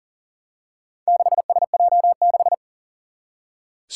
Morse code, heard as a keyed tone, about 37 wpm, 700 Hz. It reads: 6SJ6
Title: Four Challenging Letter-Number Combinations 40wpm